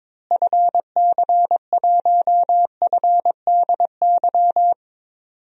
Morse code, heard as a keyed tone, about 23 wpm, 700 Hz. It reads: FC1FDY
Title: DXCC-100 Call Signs - 6 Characters - Repeated 3x - 22wpm